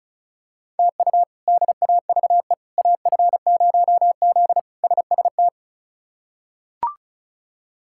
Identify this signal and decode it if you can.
Morse code, heard as a keyed tone, about 34 wpm, 700 Hz. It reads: TU DAVE AF0Z SST
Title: K1USN SST QSO Practice - 35wpm